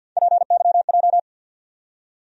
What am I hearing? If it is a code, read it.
Morse code, heard as a keyed tone, about 52 wpm, 700 Hz. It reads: PYJ
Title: USA Call Sign Postfixes - 3 Letters 50wpm